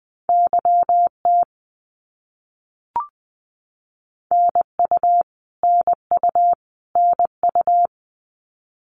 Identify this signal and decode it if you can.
Morse code, heard as a keyed tone, about 20 wpm, 700 Hz. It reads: YT NU NU NU
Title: Canadian Provinces and Territories - Repeated 3x - 20wpm